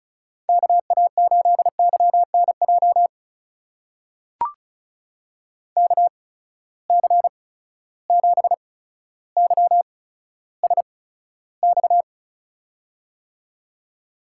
Morse code, heard as a keyed tone, about 36 wpm, spaced as for 15 wpm, 700 Hz. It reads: KA8YNJ KC7YSX
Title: USA Call Signs - 6 Characters - 8x Character Spacing - 35wpm